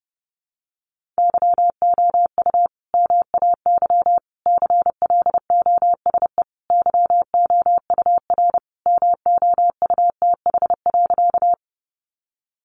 Morse code, heard as a keyed tone, about 28 wpm, 700 Hz. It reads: YOU MAY CLOSE YOUR MOUTH.
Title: Sentences from Top 1500 Words 30wpm